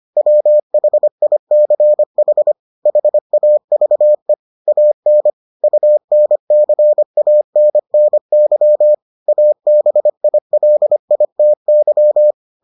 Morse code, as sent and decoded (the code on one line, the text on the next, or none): .-- .... .. -.-. .... / .... .- ...- . / .- -. / ..- -. -.-. .- -. -. -.-- / .- -... .. .-.. .. - -.--
WHICH HAVE AN UNCANNY ABILITY